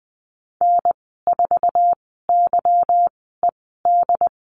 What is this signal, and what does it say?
Morse code, heard as a keyed tone, about 19 wpm, spaced as for 14 wpm, 700 Hz. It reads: N4YED
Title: 245 Lesson 39g - Letter z - Call Signs 20wpm